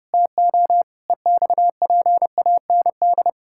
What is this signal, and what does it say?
Morse code, heard as a keyed tone, about 30 wpm, 700 Hz. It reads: TO EXPAND
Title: Pangram Challenge v2 - 30wpm